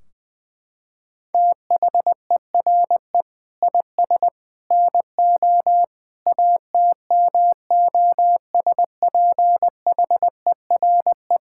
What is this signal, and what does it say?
Morse code, heard as a keyed tone, about 21 wpm, 700 Hz. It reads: THERE IS NO ATMOSPHERE
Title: Sentences from Top 1600 Words 20wpm